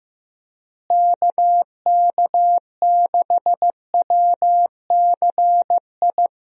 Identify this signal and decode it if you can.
Morse code, heard as a keyed tone, about 15 wpm, 700 Hz. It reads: KK6WCI
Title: USA Call Signs - 6 Characters - Repeated 2x - 15wpm